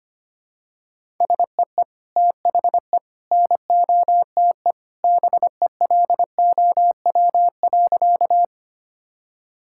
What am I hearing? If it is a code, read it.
Morse code, heard as a keyed tone, about 25 wpm, 700 Hz. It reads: SEE THE NOTE BELOW.
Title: Sentences from Top 1000 Words - Review Part 5 - 25wpm